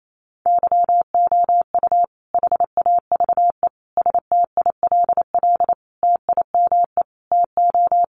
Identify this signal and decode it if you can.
Morse code, heard as a keyed tone, about 28 wpm, 700 Hz. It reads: YOU HAVE STILL TIME TO
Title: Sentences from Top 600 Words 28wpm